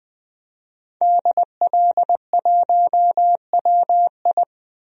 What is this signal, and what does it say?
Morse code, heard as a keyed tone, about 20 wpm, 700 Hz. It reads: DL1WI
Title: DXCC-100 Call Signs - 5 Characters - 20wpm